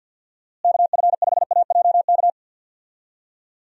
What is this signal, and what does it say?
Morse code, heard as a keyed tone, about 47 wpm, 700 Hz. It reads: KF5AJK
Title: USA Call Signs - 6 Characters - Repeated 2x - 50wpm